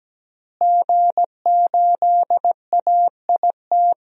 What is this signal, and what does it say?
Morse code, heard as a keyed tone, about 17 wpm, 700 Hz. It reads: G8AIT
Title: DXCC-100 Call Signs - 5 Characters - Repeated 3x - 17wpm